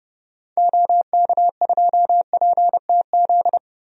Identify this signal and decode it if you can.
Morse code, heard as a keyed tone, about 30 wpm, 700 Hz. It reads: OK2PTZ
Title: DXCC-100 Call Signs - 6 Characters - 30wpm